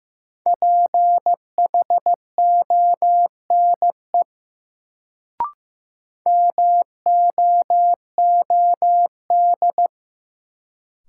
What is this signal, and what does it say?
Morse code, heard as a keyed tone, about 15 wpm, 700 Hz. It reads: PHONE MOOD
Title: Sentences from Top 1100 Words - New Word Drill Alt - 15wpm